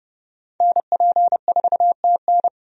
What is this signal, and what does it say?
Morse code, heard as a keyed tone, about 32 wpm, 700 Hz. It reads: NP4TN
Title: USA Call Signs - 5 Characters - Repeated 2x - 30wpm